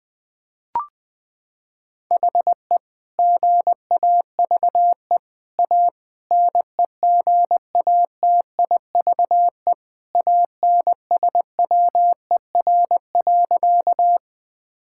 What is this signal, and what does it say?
Morse code, heard as a keyed tone, about 20 wpm, 700 Hz. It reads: HE GAVE A NEGATIVE ANSWER.
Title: Sentences from Top 1000 Words - Review Part 6 - 20wpm